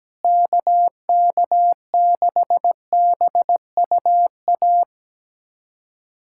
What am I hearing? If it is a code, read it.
Morse code, heard as a keyed tone, about 17 wpm, 700 Hz. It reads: KK6BUA
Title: USA Callsign DE Callsign - 6 Characters - 17wpm